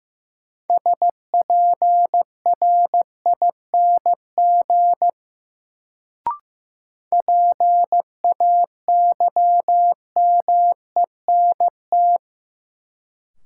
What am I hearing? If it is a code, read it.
Morse code, heard as a keyed tone, about 15 wpm, 700 Hz. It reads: SPRING PAYMENT
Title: Sentences from Top 1200 Words - New Word Drill Alt - 15wpm